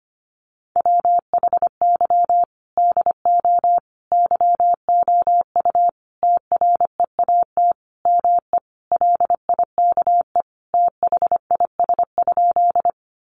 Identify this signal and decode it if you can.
Morse code, heard as a keyed tone, about 25 wpm, 700 Hz. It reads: WHY DO YOU TREAT ME LIKE THIS?
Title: Sentences from Top 1300 Words 25wpm